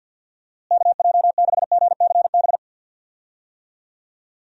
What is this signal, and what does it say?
Morse code, heard as a keyed tone, about 50 wpm, 700 Hz. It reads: KJ6GCB
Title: USA Call Signs - 6 Characters - Repeated 2x - 50wpm